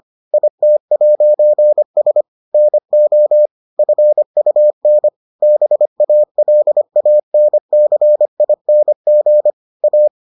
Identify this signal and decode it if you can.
Morse code, heard as a keyed tone, about 25 wpm, 600 Hz. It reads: IT'S NO FUN BALANCING A